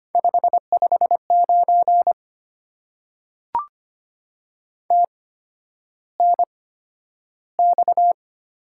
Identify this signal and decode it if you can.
Morse code, heard as a keyed tone, about 25 wpm, spaced as for 11 wpm, 700 Hz. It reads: 559 TNX
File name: Top 90 QSO Elements - 8x Character Spacing - 25wpm